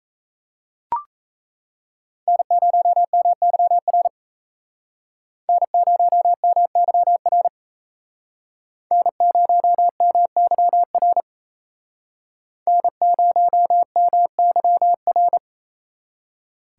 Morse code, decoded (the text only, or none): N0MYR N0MYR N0MYR N0MYR